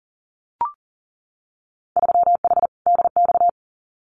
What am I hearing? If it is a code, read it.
Morse code, heard as a keyed tone, about 40 wpm, 700 Hz. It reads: 3H DX
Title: Winter Field Day Exchanges 40wpm